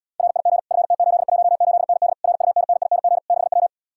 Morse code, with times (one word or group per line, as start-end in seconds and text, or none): 0.2-0.6 RIG
0.7-2.1 FT818ND
2.2-3.2 RUNNING
3.3-3.7 6W